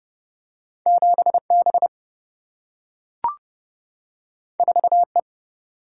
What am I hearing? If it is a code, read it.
Morse code, heard as a keyed tone, about 31 wpm, 700 Hz. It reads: 7 B 4 E
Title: Single-Letter Single-Number 30wpm